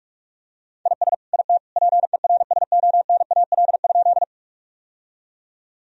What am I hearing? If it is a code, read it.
Morse code, heard as a keyed tone, about 42 wpm, 700 Hz. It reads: IS IT PERSONAL?
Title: Sentences from Top 1000 Words - Review Part 4 - 45wpm